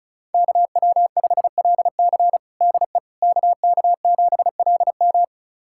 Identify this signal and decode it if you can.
Morse code, heard as a keyed tone, about 36 wpm, 700 Hz. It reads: KW5LC DE KK7LM
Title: USA Callsign DE Callsign - 5 Characters - 35wpm